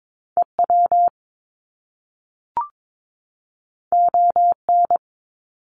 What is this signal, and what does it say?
Morse code, heard as a keyed tone, about 22 wpm, 700 Hz. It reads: EW ON
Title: Top Two Letter Combinations in Words 22wpm